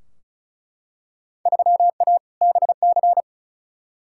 Morse code, heard as a keyed tone, about 36 wpm, 700 Hz. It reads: 3A BC
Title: Field Day Exchanges - 35wpm